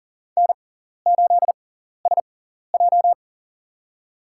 Morse code, heard as a keyed tone, about 40 wpm, spaced as for 14 wpm, 700 Hz. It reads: N8SJ
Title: USA Call Signs - 6x Character Spacing - 40wpm